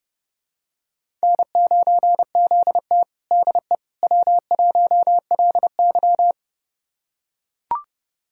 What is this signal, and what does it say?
Morse code, heard as a keyed tone, about 30 wpm, 700 Hz. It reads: N9ZT DE W1LY
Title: USA Callsign DE Callsign - 4 Characters - 30wpm